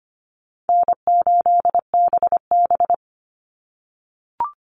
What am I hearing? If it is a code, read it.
Morse code, heard as a keyed tone, about 25 wpm, 700 Hz. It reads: N8BB
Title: NA SOTA Chaser Callsigns 25wpm